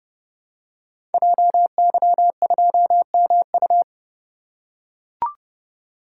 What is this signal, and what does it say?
Morse code, heard as a keyed tone, about 30 wpm, 700 Hz. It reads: JY2MU
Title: Code Groups - 5 Letter:Numbers - 30wpm